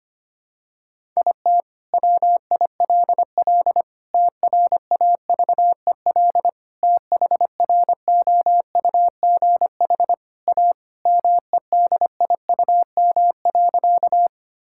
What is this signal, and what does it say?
Morse code, heard as a keyed tone, about 25 wpm, 700 Hz. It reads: IT WILL TRAVEL THROUGH A MEDIUM.
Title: Sentences from Top 1000 Words - Review Part 6 - 25wpm